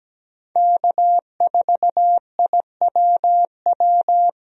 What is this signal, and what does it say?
Morse code, heard as a keyed tone, about 17 wpm, 700 Hz. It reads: K4IWW
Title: USA Call Signs - 5 Characters - 17wpm